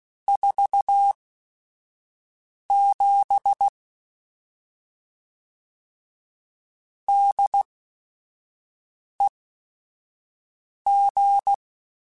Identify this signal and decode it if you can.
Morse code, heard as a keyed tone, about 16 wpm, spaced as for 5 wpm, 800 Hz. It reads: 47 DEG